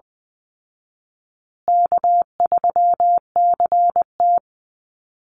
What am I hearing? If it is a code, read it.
Morse code, heard as a keyed tone, about 20 wpm, 700 Hz. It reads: K3CT
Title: Call Signs Repeated 2x - 20wpm